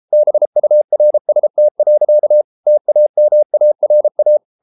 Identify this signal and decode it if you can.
Morse code, heard as a keyed tone, about 33 wpm, 600 Hz. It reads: BURST. TAMARA